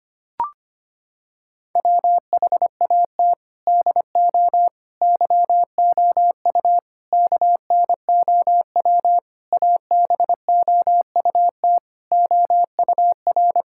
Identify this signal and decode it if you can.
Morse code, heard as a keyed tone, about 25 wpm, 700 Hz. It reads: WHAT DO YOU KNOW ABOUT OUR
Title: Sentences from Top 1000 Words - Review Part 6 - 25wpm